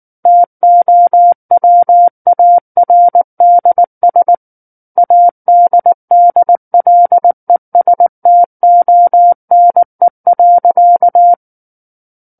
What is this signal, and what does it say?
Morse code, heard as a keyed tone, about 19 wpm, 700 Hz. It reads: TOWARDS ADDLESTONE.
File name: Book One / 10-Chapter 10 - 19 WPM